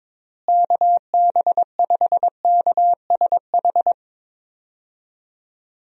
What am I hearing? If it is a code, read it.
Morse code, heard as a keyed tone, about 22 wpm, 700 Hz. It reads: KB5KSH
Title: USA Call Signs - 6 Characters - Repeated 2x - 22wpm